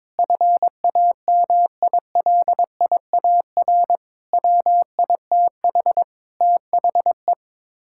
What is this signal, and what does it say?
Morse code, heard as a keyed tone, about 22 wpm, 700 Hz. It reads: FAMILIAR WITH THE
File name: practice / 22wpm / Sets of 7 Words - 22wpm